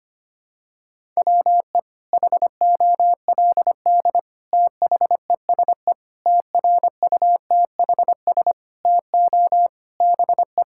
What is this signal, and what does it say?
Morse code, heard as a keyed tone, about 25 wpm, 700 Hz. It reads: WE HOLD THESE TRUTHS TO BE
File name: Sets of 9 Words - 25wpm